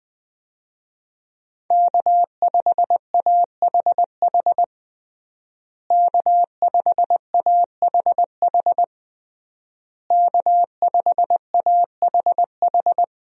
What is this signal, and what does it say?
Morse code, heard as a keyed tone, about 20 wpm, 700 Hz. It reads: K5AHH K5AHH K5AHH
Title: USA Call Signs - 5 Characters - Repeated 3x - 20wpm